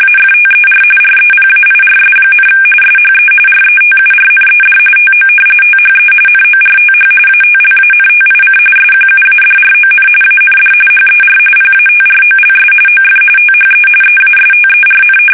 RTTY, about 45.45 baud. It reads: AFTER REPAIRS, SHE RETURNED TO THE NORTH ATLANTIC SQUADRON AND HER PATROLS OF THE EASTERN
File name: fsk850 / battleships